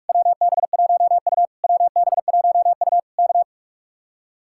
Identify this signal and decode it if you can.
Morse code, heard as a keyed tone, about 48 wpm, 700 Hz. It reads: WB1U WB1U K